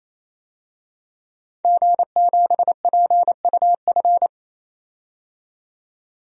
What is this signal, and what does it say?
Morse code, heard as a keyed tone, about 28 wpm, 700 Hz. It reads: G7PUF
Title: DXCC-100 Call Signs - 5 Characters - 28wpm